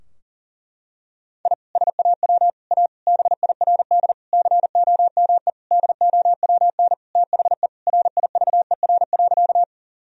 Morse code, heard as a keyed tone, about 40 wpm, 700 Hz. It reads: I SAW A BIRD COME DOWN THE RIVER.
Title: Sentences from Top 1000 Words - Review Part 2 - 40wpm